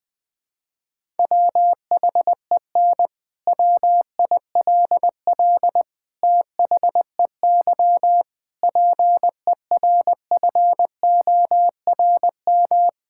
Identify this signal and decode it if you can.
Morse code, heard as a keyed tone, about 20 wpm, 700 Hz. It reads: WHEN WILL THEY PERFORM
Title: Sentences from Top 1000 Words - Review Part 5 - 20wpm